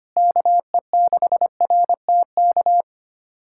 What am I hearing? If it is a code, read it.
Morse code, heard as a keyed tone, about 25 wpm, 700 Hz. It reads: KE6RTK
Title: USA Callsign DE Callsign - 6 Characters - 25wpm